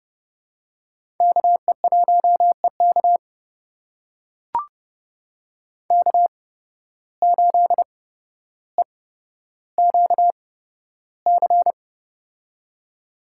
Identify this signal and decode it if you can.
Morse code, heard as a keyed tone, about 31 wpm, spaced as for 13 wpm, 700 Hz. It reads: KE1EK K8EQC
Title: USA Call Signs - 5 Characters - 8x Character Spacing - 30wpm